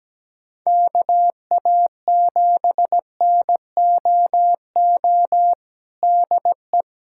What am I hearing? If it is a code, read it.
Morse code, heard as a keyed tone, about 17 wpm, 700 Hz. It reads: KA7NOO DE
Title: USA Callsign DE Callsign 17wpm